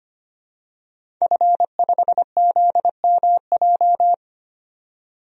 Morse code, heard as a keyed tone, about 25 wpm, 700 Hz. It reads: F5ZMJ
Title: DXCC-100 Call Signs - 5 Characters - 25wpm